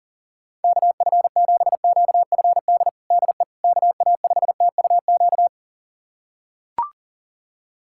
Morse code, heard as a keyed tone, about 40 wpm, 700 Hz. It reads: KF7QFD DE KA5TUQ
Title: USA Callsign DE Callsign - 6 Characters - 40wpm